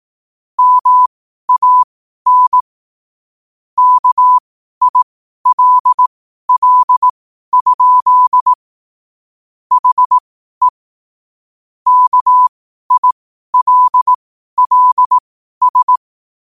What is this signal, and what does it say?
Morse code, heard as a keyed tone, about 18 wpm, spaced as for 12 wpm, 1000 Hz. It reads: MAN KILL? HE KILLS
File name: qotdfiles / 12wpm